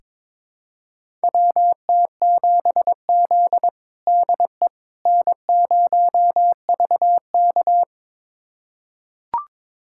Morse code, heard as a keyed tone, about 21 wpm, 700 Hz. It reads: WT7Z DE N0VK
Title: USA Callsign DE Callsign - 4 Characters - 22wpm